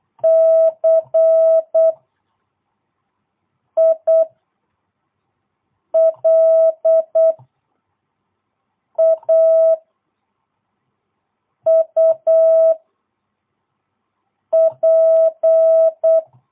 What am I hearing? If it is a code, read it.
Morse code, heard as a keyed tone, about 8 wpm, 650 Hz. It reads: C I L A U P
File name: jh8cbh / 2022cw